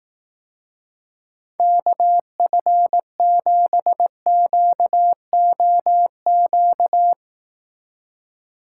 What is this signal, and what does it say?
Morse code, heard as a keyed tone, about 18 wpm, 700 Hz. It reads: KF7QOQ
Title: Speed Racing - USA 2x3 Call Signs 15wpm